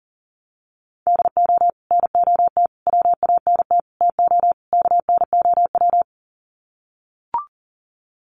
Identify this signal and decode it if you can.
Morse code, heard as a keyed tone, about 40 wpm, 700 Hz. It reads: DO NOT WANT TO KNOW
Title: Sets of 5 Words Encore 40wpm